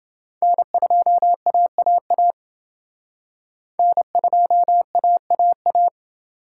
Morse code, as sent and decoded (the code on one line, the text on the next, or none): -. ..--- .- .- .- / -. ..--- .- .- .-
N2AAA N2AAA